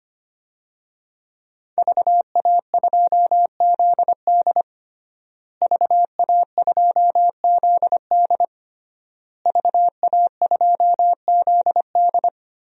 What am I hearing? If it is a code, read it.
Morse code, heard as a keyed tone, about 25 wpm, 700 Hz. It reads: VA2ZD VA2ZD VA2ZD
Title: DXCC-100 Call Signs - 5 Characters - Repeated 3x - 25wpm